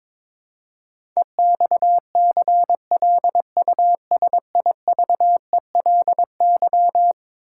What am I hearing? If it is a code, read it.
Morse code, heard as a keyed tone, about 22 wpm, 700 Hz. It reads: EXCLUSIVELY
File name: practice / 22wpm / Eleven Letter Words - 22wpm